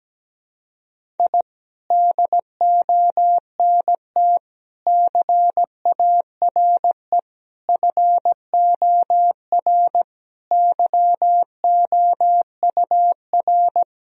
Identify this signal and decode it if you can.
Morse code, heard as a keyed tone, about 17 wpm, 700 Hz. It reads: I DONT CARE FOR YOUR
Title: Sentences from Top 1000 Words - Review Part 3 - 17wpm